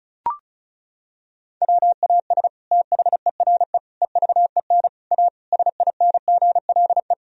Morse code, heard as a keyed tone, about 36 wpm, 700 Hz. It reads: WAS THERE EVEN A SINGLE
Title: Sentences from Top 1000 Words - Review Part 4 - 35wpm